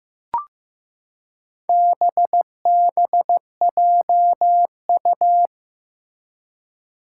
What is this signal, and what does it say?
Morse code, heard as a keyed tone, about 15 wpm, 700 Hz. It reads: BBJU
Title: Four Challenging Letter-Number Combinations 15wpm